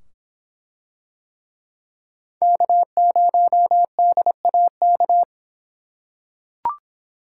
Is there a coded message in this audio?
Yes: Morse code from 2.4 to 5.2 s, heard as a keyed tone, about 25 wpm, 700 Hz. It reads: K0DAK